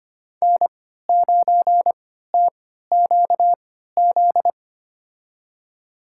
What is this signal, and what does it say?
Morse code, heard as a keyed tone, about 25 wpm, spaced as for 14 wpm, 700 Hz. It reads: N9TQZ
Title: USA Call Signs - 5 Characters - 3x Character Spacing - 25wpm